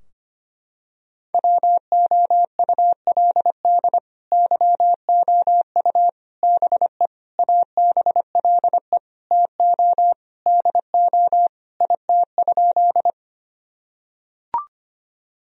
Morse code, heard as a keyed tone, about 25 wpm, 700 Hz. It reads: WOULD YOU BE ABLE TO DO IT?
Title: Sentences from Top 1000 Words - Review Part 2 - 25wpm